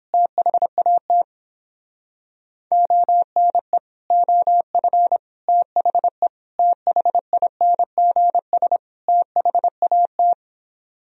Morse code, heard as a keyed tone, about 31 wpm, 700 Hz. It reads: THAT ONE OF THE THINGS THAT